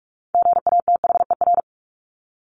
Morse code, heard as a keyed tone, about 44 wpm, 700 Hz. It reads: GATHER
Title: Top 1501-1750 Words 45wpm